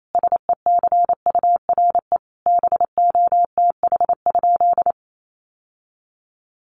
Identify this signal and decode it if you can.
Morse code, heard as a keyed tone, about 28 wpm, 700 Hz. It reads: SECURE BOTH?
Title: Sentences from Top 1200 Words 28wpm